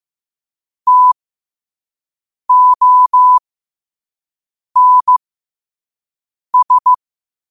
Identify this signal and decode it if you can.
Morse code, heard as a keyed tone, about 15 wpm, spaced as for 5 wpm, 1000 Hz. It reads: TONS
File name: qotdfiles / 5wpm